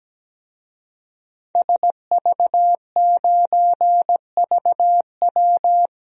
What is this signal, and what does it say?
Morse code, heard as a keyed tone, about 17 wpm, 700 Hz. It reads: SV9VW
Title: DXCC-100 Call Signs - 5 Characters - Repeated 3x - 17wpm